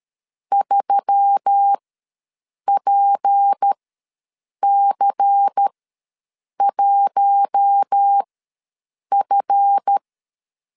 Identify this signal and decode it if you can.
Morse code, heard as a keyed tone, about 13 wpm, spaced as for 7 wpm, 800 Hz. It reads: 3PC1F
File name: LEZIONI / 8 YJ45 / 8 YJ ws 12wpm 3342